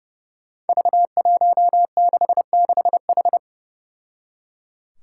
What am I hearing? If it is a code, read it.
Morse code, heard as a keyed tone, about 30 wpm, 700 Hz. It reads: V166H
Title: Five Challenging Letter-Number Combinations 30wpm